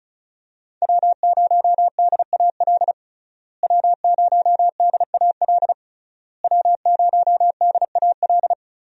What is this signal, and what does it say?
Morse code, heard as a keyed tone, about 36 wpm, 700 Hz. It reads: W0DAL W0DAL W0DAL
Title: USA Call Signs - 5 Characters - Repeated 3x - 35wpm